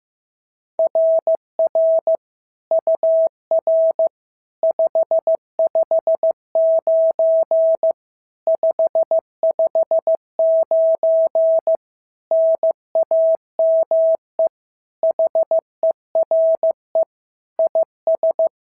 Morse code, heard as a keyed tone, about 15 wpm, 650 Hz. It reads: RR UR 559 559 NAME HERE IS
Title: SOTA QSO Practice 15wpm